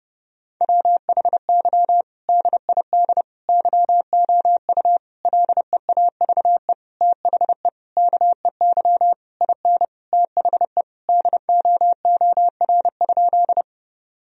Morse code, heard as a keyed tone, about 30 wpm, 700 Hz. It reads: WHY DID YOU LEAVE THE KEY IN THE DOOR?
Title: Sentences from Top 1000 Words - Review Part 4 - 30wpm